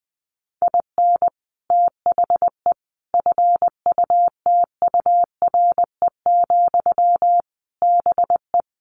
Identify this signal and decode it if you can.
Morse code, heard as a keyed tone, about 20 wpm, 700 Hz. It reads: IN THE FUTURE, BE
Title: Sentences from Top 1300 Words 20wpm